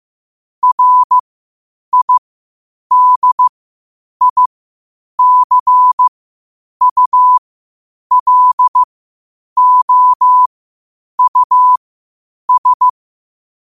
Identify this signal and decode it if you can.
Morse code, heard as a keyed tone, about 15 wpm, spaced as for 8 wpm, 1000 Hz. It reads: RIDICULOUS
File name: qotdfiles / 8wpm